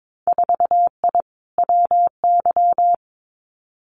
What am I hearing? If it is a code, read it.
Morse code, heard as a keyed tone, about 22 wpm, 700 Hz. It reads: 4I WY
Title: Winter Field Day Exchanges Encore 22wpm